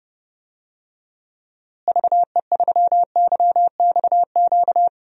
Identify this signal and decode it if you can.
Morse code, heard as a keyed tone, about 30 wpm, 700 Hz. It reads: VE3YXQ
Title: DXCC-100 Call Signs - 6 Characters - Repeated 3x - 30wpm